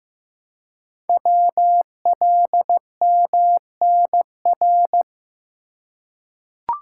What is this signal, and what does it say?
Morse code, heard as a keyed tone, about 15 wpm, 700 Hz. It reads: WLMNR
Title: Code Groups - 5 Letters - 15wpm